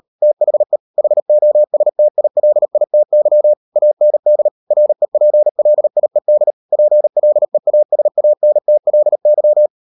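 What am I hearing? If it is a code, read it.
Morse code, heard as a keyed tone, about 38 wpm, 600 Hz. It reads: THE HOSTILITY AND REPLIED PLEASANTLY
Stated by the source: Morse code practice at 38 WPM, sending Ken Follett's novel Never in English, from CW med Gnister.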